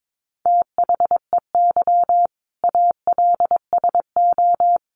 Morse code, heard as a keyed tone, about 22 wpm, 700 Hz. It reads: THEY ALSO
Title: Easy Phrases 22wpm